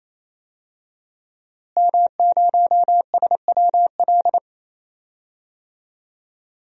Morse code, heard as a keyed tone, about 28 wpm, 700 Hz. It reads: M0SWL
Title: DXCC-100 Call Signs - 5 Characters - Repeated 3x - 28wpm